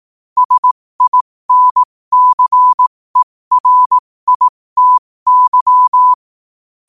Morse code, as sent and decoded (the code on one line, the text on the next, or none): ... .. -. -.-. .-. .. - -.--
SINCERITY